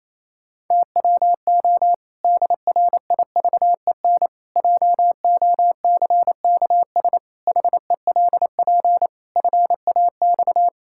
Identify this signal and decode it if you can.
Morse code, heard as a keyed tone, about 28 wpm, 700 Hz. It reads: TWO DRIVEN JOCKS HELP FAX
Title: Pangram Challenge v2 - 28wpm